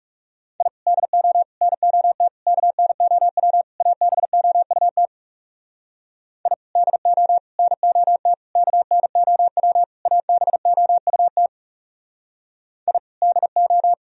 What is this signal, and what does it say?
Morse code, heard as a keyed tone, about 39 wpm, 700 Hz. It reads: I DO NOT KNOW ABOUT I DO NOT KNOW ABOUT I DO